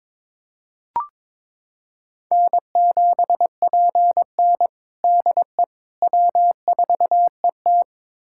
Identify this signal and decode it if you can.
Morse code, heard as a keyed tone, about 22 wpm, 700 Hz. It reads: N7PN DE W4ET
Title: USA Callsign DE Callsign - 4 Characters - 22wpm